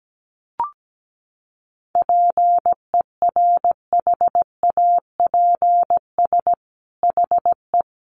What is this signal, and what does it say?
Morse code, heard as a keyed tone, about 17 wpm, 700 Hz. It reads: PERHAPS HE
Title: Sentences from Top 1100 Words 17wpm